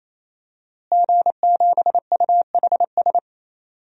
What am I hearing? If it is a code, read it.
Morse code, heard as a keyed tone, about 27 wpm, 700 Hz. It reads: G7UHS
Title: DXCC-100 Call Signs - 5 Characters - Repeated 3x - 28wpm